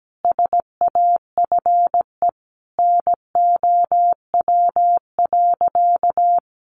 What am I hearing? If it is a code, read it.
Morse code, heard as a keyed tone, about 17 wpm, 700 Hz. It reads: SAFE NOW.
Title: Sentences from Top 900 Words 17wpm